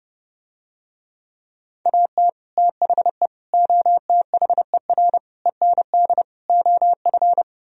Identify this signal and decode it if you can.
Morse code, heard as a keyed tone, about 30 wpm, 700 Hz. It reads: AT THE OTHER END OF